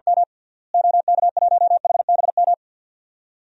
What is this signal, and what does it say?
Morse code, heard as a keyed tone, about 50 wpm, 700 Hz. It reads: M OK1HBM
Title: DXCC-100 Call Signs - 6 Characters - Repeated 2x - 50wpm